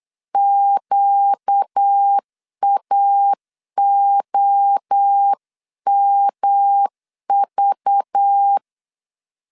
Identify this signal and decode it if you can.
Morse code, heard as a keyed tone, about 8 wpm, 800 Hz. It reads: QAOMV
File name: 8wpm